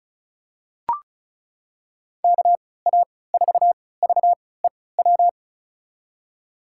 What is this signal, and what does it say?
Morse code, heard as a keyed tone, about 34 wpm, spaced as for 20 wpm, 700 Hz. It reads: KA4VEW
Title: USA Call Signs - 3x Character Spacing - 35wpm